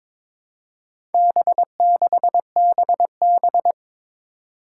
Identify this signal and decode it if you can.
Morse code, heard as a keyed tone, about 23 wpm, 700 Hz. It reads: B6BB
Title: Four Challenging Letter-Number Combinations 22wpm